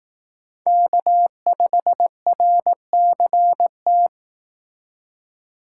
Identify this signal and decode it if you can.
Morse code, heard as a keyed tone, about 18 wpm, 700 Hz. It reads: K5RCT
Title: Speed Racing - USA Call Signs 15wpm